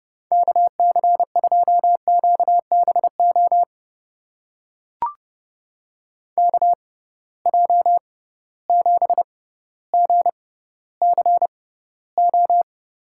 Morse code, heard as a keyed tone, about 30 wpm, spaced as for 15 wpm, 700 Hz. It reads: KC2QBO KJ7GCO
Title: USA Call Signs - 6 Characters - 6x Character Spacing - 30wpm